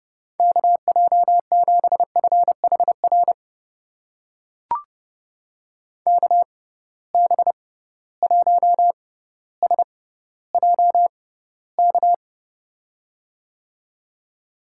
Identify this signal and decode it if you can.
Morse code, heard as a keyed tone, about 30 wpm, spaced as for 15 wpm, 700 Hz. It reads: KJ7FHR KB1SJK
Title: USA Call Signs - 6 Characters - 6x Character Spacing - 30wpm